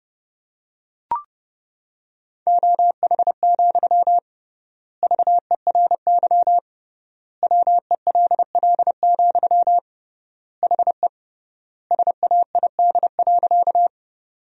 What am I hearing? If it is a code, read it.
Morse code, heard as a keyed tone, about 29 wpm, 700 Hz. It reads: OH, VERY WELL, HE SAID.
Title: 3x Word-Spacing - Sentences from Top 500 Words - 30wpm